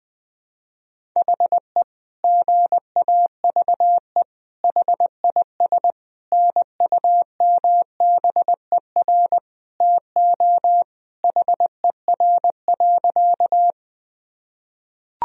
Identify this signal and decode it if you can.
Morse code, heard as a keyed tone, about 20 wpm, 700 Hz. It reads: HE GAVE HIS NUMBER TO HER.
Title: Sentences from Top 1000 Words - Review Part 1 - 20wpm